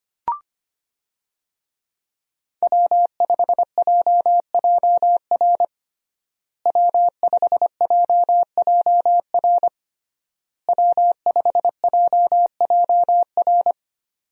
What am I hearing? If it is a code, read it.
Morse code, heard as a keyed tone, about 25 wpm, 700 Hz. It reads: W5JJR W5JJR W5JJR
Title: USA Call Signs - 5 Characters - Repeated 3x - 25wpm